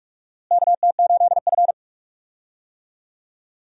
Morse code, heard as a keyed tone, about 46 wpm, 700 Hz. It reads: KT8F
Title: USA Call Signs - 4 Characters - Repeated 3x - 45wpm